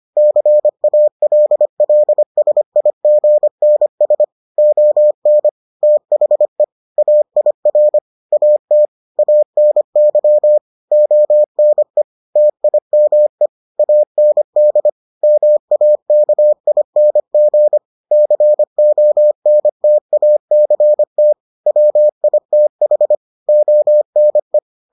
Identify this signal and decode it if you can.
Morse code, heard as a keyed tone, about 25 wpm, 600 Hz. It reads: CALLSIGNS ON THE AIR AT ANY ONE TIME AND MAKING CONTACT WITH ONE